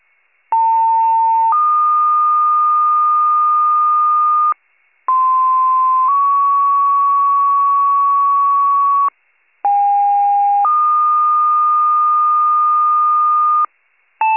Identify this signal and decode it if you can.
Two-tone page, heard as a one-second tone, then a three-second tone, about 900 then 1220 Hz, about 1040 then 1120 Hz, about 810 then 1220 Hz. It is heard from 0.5 to 13.7 s.